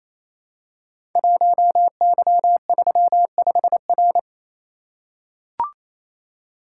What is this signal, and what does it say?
Morse code, heard as a keyed tone, about 29 wpm, 700 Hz. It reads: 1Y35R
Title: Code Groups - 5 Letter:Numbers - 28wpm